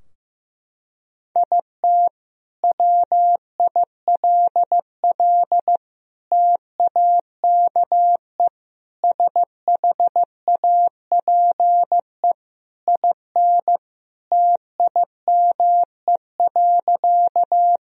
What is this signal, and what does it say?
Morse code, heard as a keyed tone, about 15 wpm, 700 Hz. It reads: IT WILL TAKE SHAPE IN TIME.
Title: Sentences from Top 1000 Words - Review Part 3 - 15wpm